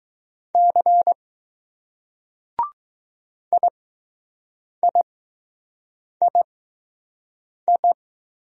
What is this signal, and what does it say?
Morse code, heard as a keyed tone, about 20 wpm, spaced as for 6 wpm, 700 Hz. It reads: C IIII